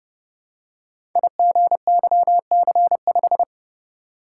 Morse code, heard as a keyed tone, about 30 wpm, 700 Hz. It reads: IGYC5
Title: Code Groups - 5 Letter:Numbers - 30wpm